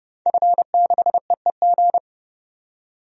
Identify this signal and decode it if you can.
Morse code, heard as a keyed tone, about 32 wpm, 700 Hz. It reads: F6EEG
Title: DXCC-100 Call Signs - 5 Characters - 30wpm